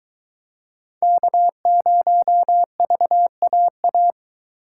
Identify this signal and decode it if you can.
Morse code, heard as a keyed tone, about 23 wpm, 700 Hz. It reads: K0VAA